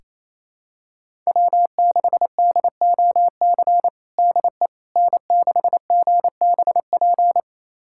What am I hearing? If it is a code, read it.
Morse code, heard as a keyed tone, about 28 wpm, 700 Hz. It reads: W6DOC DE N6GBP
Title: USA Callsign DE Callsign - 5 Characters - 28wpm